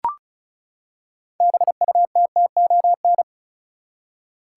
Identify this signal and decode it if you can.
Morse code, heard as a keyed tone, about 34 wpm, 700 Hz. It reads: BUTTON
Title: Six Letter Words - 35wpm